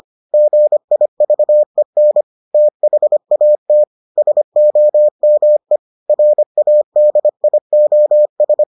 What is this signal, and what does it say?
Morse code, heard as a keyed tone, about 25 wpm, 600 Hz. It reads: GIVEN THAT SOME RADIOS